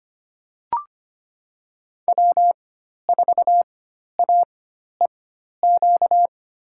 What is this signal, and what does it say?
Morse code, heard as a keyed tone, about 25 wpm, spaced as for 12 wpm, 700 Hz. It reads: W4AEQ
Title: USA Call Signs - 5 Characters - 4x Character Spacing - 25wpm